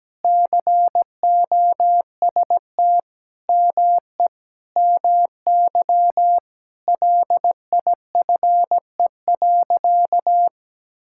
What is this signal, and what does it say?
Morse code, heard as a keyed tone, about 17 wpm, 700 Hz. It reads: COST ME MY LIFE.
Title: Sentences from Top 1000 Words - Review Part 5 - 17wpm